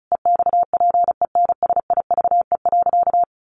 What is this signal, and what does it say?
Morse code, heard as a keyed tone, about 34 wpm, 700 Hz. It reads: EXPENSIVE.
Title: Sentences from Top 1500 Words 35wpm